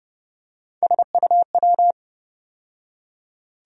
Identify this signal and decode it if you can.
Morse code, heard as a keyed tone, about 30 wpm, 700 Hz. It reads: SUW
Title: USA Call Sign Postfixes - 3 Letters 30wpm